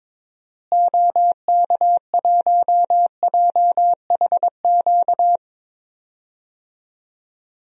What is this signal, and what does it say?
Morse code, heard as a keyed tone, about 22 wpm, 700 Hz. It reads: OK1JHQ
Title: DXCC-100 Call Signs - 6 Characters - Repeated 2x - 22wpm